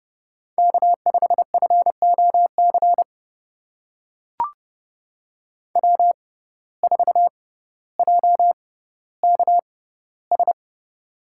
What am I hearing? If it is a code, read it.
Morse code, heard as a keyed tone, about 30 wpm, spaced as for 15 wpm, 700 Hz. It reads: K5FOC W4JKS
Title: USA Call Signs - 5 Characters - 6x Character Spacing - 30wpm